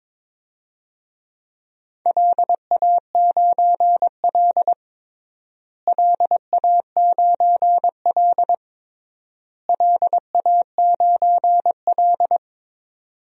Morse code, heard as a keyed tone, about 22 wpm, 700 Hz. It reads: LA9L LA9L LA9L
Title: DXCC-100 Call Signs - 4 Characters - Repeated 3x - 22wpm